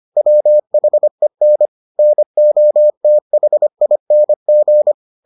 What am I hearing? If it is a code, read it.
Morse code, heard as a keyed tone, about 25 wpm, 600 Hz. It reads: WHEN NOTHING